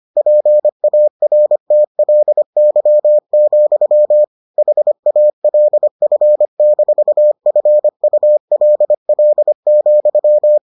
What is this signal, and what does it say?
Morse code, heard as a keyed tone, about 25 wpm, 600 Hz. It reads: PARTLY, HALF-FULL,